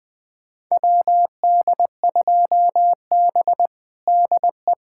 Morse code, heard as a keyed tone, about 20 wpm, 700 Hz. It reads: WD2B DE
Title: USA Callsign DE Callsign - 4 Characters - 20wpm